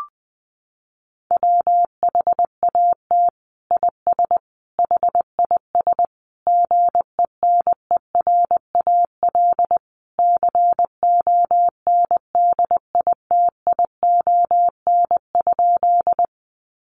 Morse code, heard as a keyed tone, about 20 wpm, 700 Hz. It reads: WHAT IS HIS GENERAL CONDITION?
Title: Sentences from Top 1000 Words 20wpm